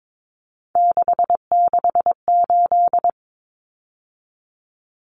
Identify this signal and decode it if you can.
Morse code, heard as a keyed tone, about 22 wpm, 700 Hz. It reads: 668
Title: Common Cut Numbers - 3-Digit 22wpm